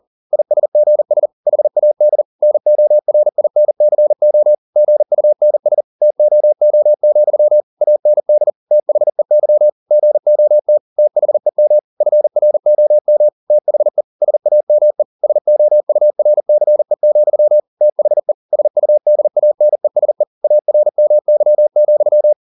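Morse code, heard as a keyed tone, about 40 wpm, 600 Hz. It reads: ISGS HAD NORINCO GUNS TOO, AND THEY GOT THEM FROM THE SAME SOURCE, THE SUDANESE ARMY,
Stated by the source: Morse code at 40 wpm, practice text, CW med Gnister.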